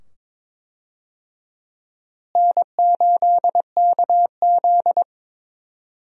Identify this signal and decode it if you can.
Morse code, heard as a keyed tone, about 22 wpm, 700 Hz. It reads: N8KZ
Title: USA Call Signs - 4 Characters - Repeated 2x - 22wpm